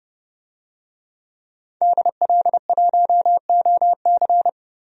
Morse code, heard as a keyed tone, about 30 wpm, 700 Hz. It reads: DL1OC
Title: DXCC-100 Call Signs - 5 Characters - 30wpm